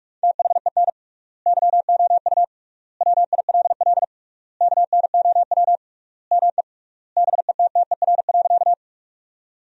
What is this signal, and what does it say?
Morse code, heard as a keyed tone, about 46 wpm, 700 Hz. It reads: THEN YOU WILL KNOW ME BETTER.
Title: 3x Word-Spacing - Sentences from Top 400 Words - 45wpm